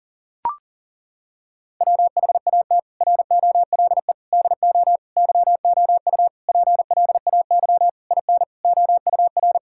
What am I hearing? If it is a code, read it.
Morse code, heard as a keyed tone, about 40 wpm, 700 Hz. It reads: WHAT ROLE DO YOU PLAY IN OUR
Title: Sentences from Top 1000 Words - Review Part 6 - 40wpm